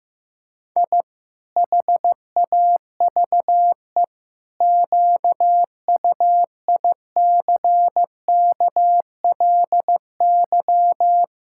Morse code, heard as a keyed tone, about 15 wpm, 700 Hz. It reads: I HAVE QUICKLY
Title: Pangram Challenge v2 - 15wpm